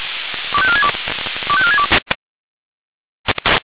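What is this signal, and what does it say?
Five-tone selective call (ZVEI), heard as quick runs of five tones, tones of about 70 ms. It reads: 25462 25462
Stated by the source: Five-tone ZVEI selective call received on SDR.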